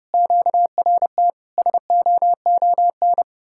Morse code, heard as a keyed tone, about 30 wpm, 700 Hz. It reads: QRT SOON